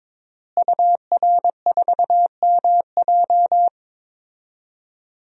Morse code, heard as a keyed tone, about 22 wpm, 700 Hz. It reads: UR4MJ
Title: DXCC-100 Call Signs - 5 Characters - Repeated 2x - 22wpm